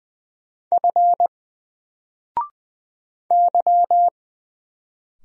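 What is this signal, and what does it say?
Morse code, heard as a keyed tone, about 20 wpm, 700 Hz. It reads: F Y
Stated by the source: Single letter and number Morse drill at 20 wpm, from N4LCD.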